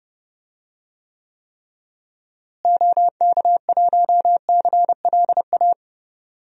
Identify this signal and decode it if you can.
Morse code, heard as a keyed tone, about 29 wpm, 700 Hz. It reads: OK1CLA
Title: DXCC-100 Call Signs - 6 Characters - Repeated 3x - 30wpm